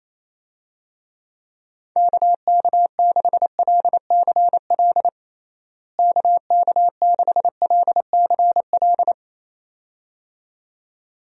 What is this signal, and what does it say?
Morse code, heard as a keyed tone, about 27 wpm, 700 Hz. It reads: KK6LCL KK6LCL
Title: USA Call Signs - 6 Characters - Repeated 2x - 28wpm